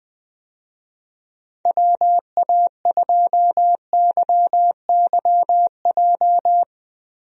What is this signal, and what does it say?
Morse code, heard as a keyed tone, about 20 wpm, 700 Hz. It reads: WA2YYJ
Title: USA Call Signs - 6 Characters - 20wpm